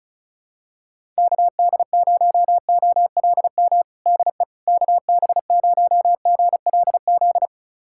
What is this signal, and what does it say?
Morse code, heard as a keyed tone, about 36 wpm, 700 Hz. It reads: KD0OLM DE KB0GLZ
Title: USA Callsign DE Callsign - 6 Characters - 35wpm